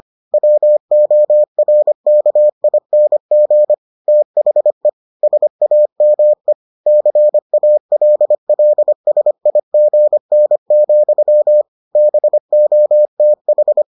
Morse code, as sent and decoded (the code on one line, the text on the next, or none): .-- --- .-. -.- .. -. --. / - .... / ... .- -- . / -.-. .- .-.. .-.. ... .. --. -. --..-- / -... --- - ....
WORKING THE SAME CALLSIGN, BOTH